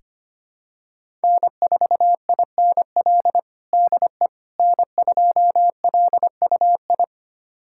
Morse code, heard as a keyed tone, about 25 wpm, 700 Hz. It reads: N4INL DE N2LUI
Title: USA Callsign DE Callsign - 5 Characters - 25wpm